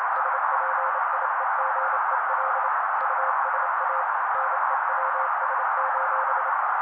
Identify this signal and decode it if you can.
Morse code, heard as a keyed tone, about 28 wpm, 530 Hz. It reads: SPIEGEL USA NEWS:
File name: daily / 27wpm noise